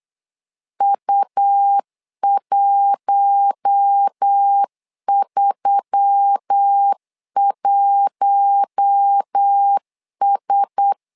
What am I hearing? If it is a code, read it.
Morse code, heard as a keyed tone, about 8 wpm, 800 Hz. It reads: U131S
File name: LEZIONI / 4 Q1S3 8wpm 3037